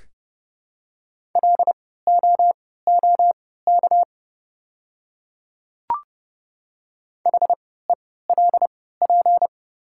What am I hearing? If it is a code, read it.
Morse code, heard as a keyed tone, about 30 wpm, spaced as for 17 wpm, 700 Hz. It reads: LOOK HELP
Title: 3x Character Spacing - Top 200 Words - 30wpm